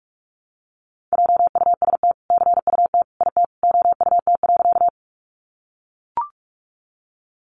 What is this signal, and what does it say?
Morse code, heard as a keyed tone, about 44 wpm, 700 Hz. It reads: JUST CUT IT OUT.
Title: Sentences from Top 300 Words 45wpm